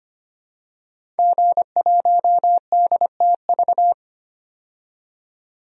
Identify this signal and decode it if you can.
Morse code, heard as a keyed tone, about 25 wpm, 700 Hz. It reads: G1DTV
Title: DXCC-100 Call Signs - 5 Characters - 25wpm